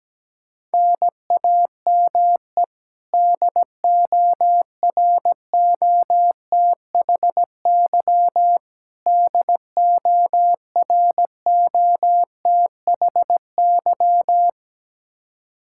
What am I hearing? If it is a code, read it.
Morse code, heard as a keyed tone, about 17 wpm, 700 Hz. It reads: NAME DOROTHY DOROTHY
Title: Practice QSO - Exchange 1 - Beginning - 17wpm